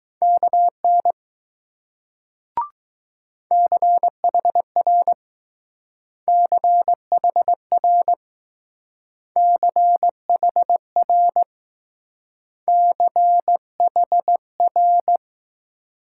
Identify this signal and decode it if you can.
Morse code, heard as a keyed tone, about 20 wpm, 700 Hz. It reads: KN CHR CHR CHR CHR